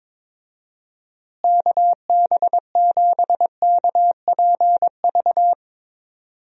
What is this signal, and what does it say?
Morse code, heard as a keyed tone, about 22 wpm, 700 Hz. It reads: KB7KPV
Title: USA Call Signs - 6 Characters - Repeated 3x - 22wpm